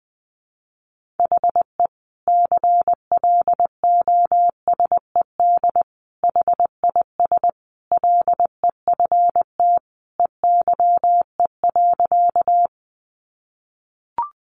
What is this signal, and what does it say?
Morse code, heard as a keyed tone, about 20 wpm, 700 Hz. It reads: HE CLOSED HIS LEFT EYE.
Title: Sentences from Top 1400 Words 20wpm